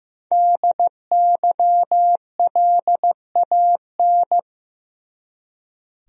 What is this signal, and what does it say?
Morse code, heard as a keyed tone, about 15 wpm, 700 Hz. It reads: DYLAN
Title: USA Common Names - 15wpm